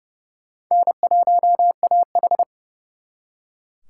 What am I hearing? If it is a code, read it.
Morse code, heard as a keyed tone, about 38 wpm, 700 Hz. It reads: N1AH